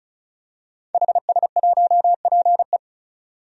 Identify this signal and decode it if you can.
Morse code, heard as a keyed tone, about 34 wpm, 700 Hz. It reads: HS1PE
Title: DXCC-100 Call Signs - 5 Characters - 35wpm